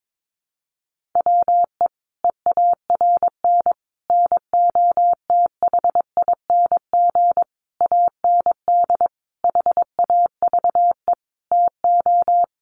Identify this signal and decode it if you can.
Morse code, heard as a keyed tone, about 22 wpm, 700 Hz. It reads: WE EARN NOTHING AND HAVE TO
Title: Sentences from Top 1500 Words 22wpm